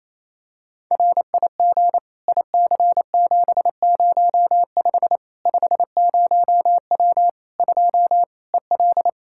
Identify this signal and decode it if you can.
Morse code, heard as a keyed tone, about 28 wpm, 700 Hz. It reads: RIG IC705 50W 2 EL
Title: Practice QSO - Exchange 2 - Info - 28wpm